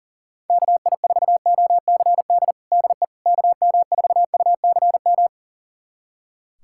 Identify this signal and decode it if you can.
Morse code, heard as a keyed tone, about 40 wpm, 700 Hz. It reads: KI4OCD DE KM4UCM
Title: USA Callsign DE Callsign - 6 Characters - 40wpm